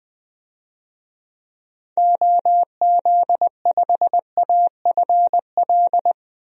Morse code, heard as a keyed tone, about 20 wpm, 700 Hz. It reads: OZ5AFL
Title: DXCC-100 Call Signs - 6 Characters - Repeated 3x - 20wpm